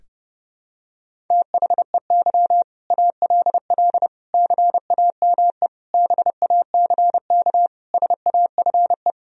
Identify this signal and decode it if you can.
Morse code, heard as a keyed tone, about 30 wpm, 700 Hz. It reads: THEY ALL CAME BACK SAFE
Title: Sentences from Top 1000 Words - Review Part 1 - 30wpm